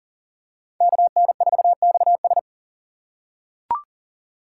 Morse code, heard as a keyed tone, about 40 wpm, 700 Hz. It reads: KN4XS
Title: USA Call Signs - 5 Characters - 4x Character Spacing - 40wpm